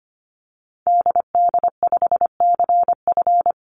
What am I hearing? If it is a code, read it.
Morse code, heard as a keyed tone, about 25 wpm, 700 Hz. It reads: DD5CF
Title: Call Signs Repeated 2x - 25wpm